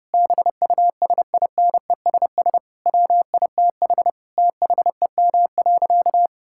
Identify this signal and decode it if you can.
Morse code, heard as a keyed tone, about 30 wpm, 700 Hz. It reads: BUSINESS WITH THEM.
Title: Sentences from Top 1000 Words - Review Part 6 - 30wpm